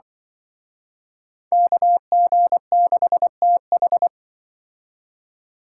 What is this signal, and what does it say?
Morse code, heard as a keyed tone, about 28 wpm, 700 Hz. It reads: KG6TH